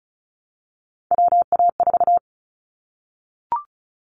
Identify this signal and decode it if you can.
Morse code, heard as a keyed tone, about 34 wpm, 700 Hz. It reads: WA4
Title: Callsign Prefixes for Top 25 DXCC - Repeated 3x - 35wpm